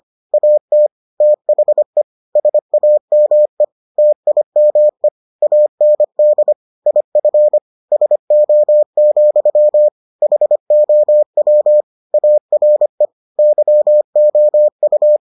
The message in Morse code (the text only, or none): AT THE SAME TIME AND IF SO, HOW ARE YOU